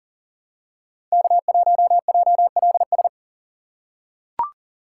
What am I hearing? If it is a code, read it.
Morse code, heard as a keyed tone, about 40 wpm, 700 Hz. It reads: K1JLS
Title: USA Call Signs - 5 Characters - 6x Character Spacing - 40wpm